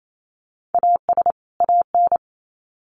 Morse code, heard as a keyed tone, about 28 wpm, 700 Hz. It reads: AS AN
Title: Sets of 2 Words 28wpm